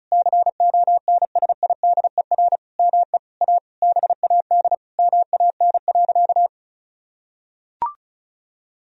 Morse code, heard as a keyed tone, about 34 wpm, 700 Hz. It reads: CONSIDER ME A BAD MAN.
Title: Sentences from Top 1000 Words - Review Part 4 - 35wpm